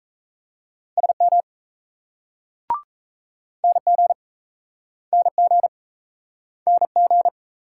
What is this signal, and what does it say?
Morse code, heard as a keyed tone, about 40 wpm, 700 Hz. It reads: SM NG NG NG